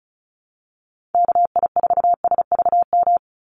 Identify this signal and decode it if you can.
Morse code, heard as a keyed tone, about 34 wpm, 700 Hz. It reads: KI4SVM
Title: NA SOTA Activator Callsigns 35wpm